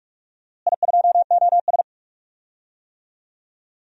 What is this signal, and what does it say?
Morse code, heard as a keyed tone, about 44 wpm, 690 Hz. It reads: I2OS
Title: DXCC-100 Call Signs - 4 Characters - Repeated 3x - 45wpm